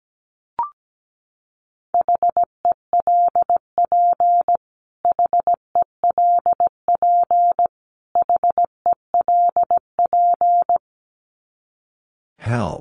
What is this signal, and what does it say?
Morse code, heard as a keyed tone, about 17 wpm, 700 Hz. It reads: HELP HELP HELP
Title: Top 300 Words Repeated 17wpm